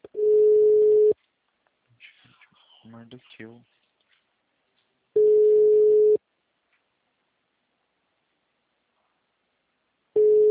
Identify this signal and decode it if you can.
Ringback tone (Europe, 425 Hz), with speech.